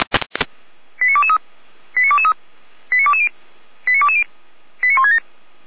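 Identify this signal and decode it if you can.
Five-tone selective call (ZVEI), heard as quick runs of five tones, tones of about 70 ms. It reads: 89223 89223 89220 89220 89167
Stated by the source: ZVEI five-tone call received by SDR.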